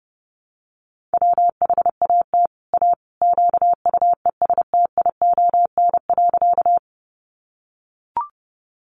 Morse code, heard as a keyed tone, about 30 wpm, 700 Hz. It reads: WHAT A QUESTION.
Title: Sentences from Top 400 Words 30wpm